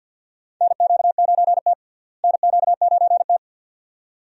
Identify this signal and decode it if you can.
Morse code, heard as a keyed tone, about 47 wpm, 700 Hz. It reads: NQ9T NQ9T
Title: USA Call Signs - 4 Characters - Repeated 2x - 50wpm